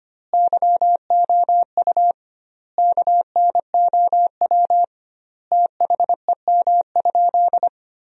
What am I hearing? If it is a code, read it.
Morse code, heard as a keyed tone, about 25 wpm, 700 Hz. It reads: YOU KNOW THEM?
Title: 2x Word-Spacing - Sentences from Top 100 Words - 25wpm